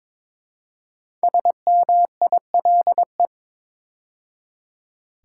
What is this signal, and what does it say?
Morse code, heard as a keyed tone, about 22 wpm, 700 Hz. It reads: SMILE
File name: 5 Letter Words - First 3 Letters Given - 22wpm